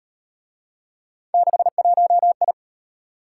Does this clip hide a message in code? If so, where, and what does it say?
Yes: Morse code from 1.3 to 2.5 s, heard as a keyed tone, about 38 wpm, 700 Hz. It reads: 61I